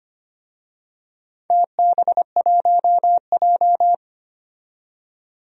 Morse code, heard as a keyed tone, about 25 wpm, 700 Hz. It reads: TB1J
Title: DXCC-100 Call Signs - 4 Characters - 25wpm